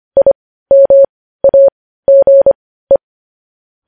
Morse code, heard as a keyed tone, about 25 wpm, spaced as for 15 wpm, 550 Hz. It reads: IMAGE
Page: News headlines at 15